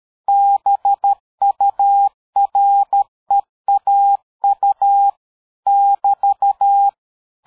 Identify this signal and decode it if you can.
Morse code, heard as a keyed tone, about 13 wpm, 800 Hz. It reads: BUREAU <BT>
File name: dk5ke / QSO 60BpM